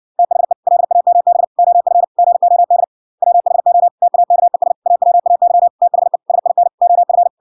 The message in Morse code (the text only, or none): THE LAMB OF GOD WHO TAKES AWAY THE SIN OF